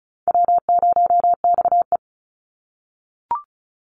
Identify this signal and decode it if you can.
Morse code, heard as a keyed tone, about 34 wpm, 700 Hz. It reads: W0XE
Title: Call Signs - 35wpm